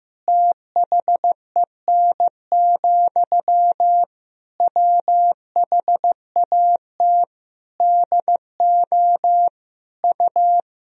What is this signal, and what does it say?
Morse code, heard as a keyed tone, about 15 wpm, 700 Hz. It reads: THEN, WHAT DO U
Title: Sentences from Top 200 Words and Abbreviations 15wpm